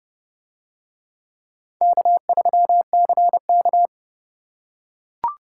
Morse code, heard as a keyed tone, about 27 wpm, 700 Hz. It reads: K3CK